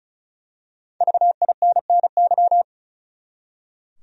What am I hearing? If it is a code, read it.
Morse code, heard as a keyed tone, about 36 wpm, 700 Hz. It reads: VINNY
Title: USA Common Names - Max 5 Letters - 35wpm